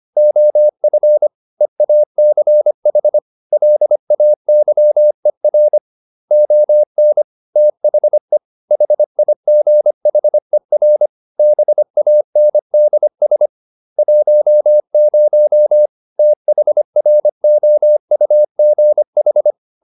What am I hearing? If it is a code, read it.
Morse code, heard as a keyed tone, about 25 wpm, 600 Hz. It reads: OF EACH LAYER ON THE HIGHER BANDS 10 THROUGH